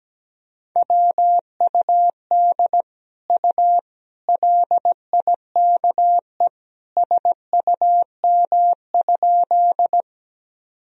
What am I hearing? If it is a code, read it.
Morse code, heard as a keyed tone, about 17 wpm, 700 Hz. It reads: WUD U LIKE SUM?
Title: Sentences from Top 100 Words and Abbreviations 17wpm